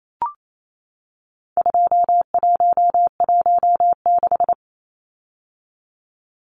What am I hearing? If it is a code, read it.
Morse code, heard as a keyed tone, about 28 wpm, 700 Hz. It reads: 2116
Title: Numbers - 4-Digit 28wpm